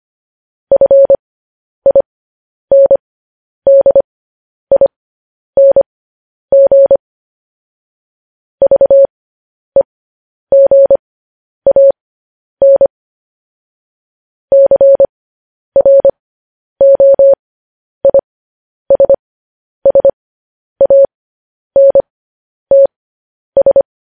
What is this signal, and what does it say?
Morse code, heard as a keyed tone, about 25 wpm, spaced as for 10 wpm, 550 Hz. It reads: FINDING VEGAN CROISSANTS